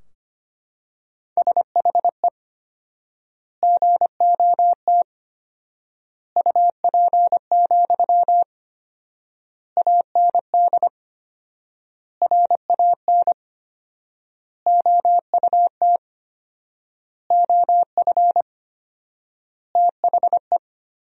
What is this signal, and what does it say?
Morse code, heard as a keyed tone, about 25 wpm, 700 Hz. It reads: SHE GOT UP, AND RAN OUT OF THE
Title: 4x Word-Spacing - Sentences from Top 500 Words - 25wpm